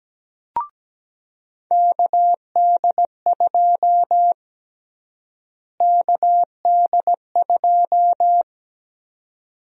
Morse code, heard as a keyed tone, about 17 wpm, 700 Hz. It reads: KD2 KD2
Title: USA Call Sign Prefixes - Repeated 2x - 17wpm